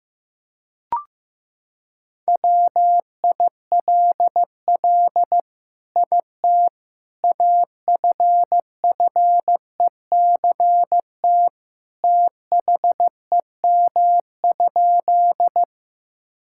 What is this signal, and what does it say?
Morse code, heard as a keyed tone, about 16 wpm, 700 Hz. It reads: WILL IT AFFECT THEM?
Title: Sentences from Top 1000 Words - Review Part 6 - 15wpm